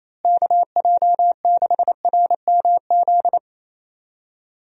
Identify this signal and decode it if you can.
Morse code, heard as a keyed tone, about 28 wpm, 700 Hz. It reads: KJ6RMZ
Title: USA Call Signs - 6 Characters - Repeated 3x - 28wpm